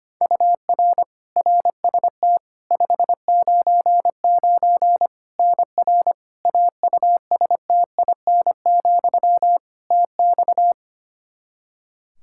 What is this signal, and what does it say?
Morse code, heard as a keyed tone, about 25 wpm, 700 Hz. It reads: UR RST 599 NR AUSTIN, TX
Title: Practice QSO - Exchange 1 - Beginning - 25wpm